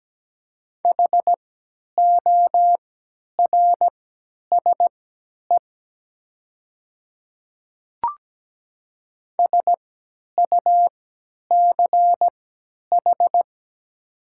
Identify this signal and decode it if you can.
Morse code, heard as a keyed tone, about 17 wpm, spaced as for 10 wpm, 700 Hz. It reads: HORSE SUCH
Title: 3x Character Spacing - Top 300 Words - 17wpm